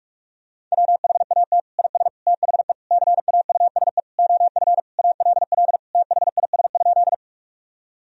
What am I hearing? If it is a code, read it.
Morse code, heard as a keyed tone, about 42 wpm, 700 Hz. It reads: WHAT IS THE CAUSE OF ALL THIS?
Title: Sentences from Top 1000 Words - Review Part 1 - 45wpm